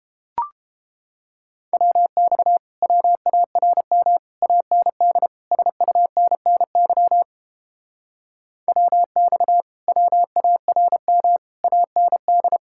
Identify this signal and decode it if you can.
Morse code, heard as a keyed tone, about 31 wpm, 700 Hz. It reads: WX WARM AND SUNNY WX WARM AND